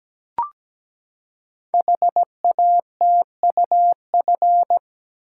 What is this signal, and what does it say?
Morse code, heard as a keyed tone, about 17 wpm, 700 Hz. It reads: HATUF
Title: Code Groups - 5 Letters - 17wpm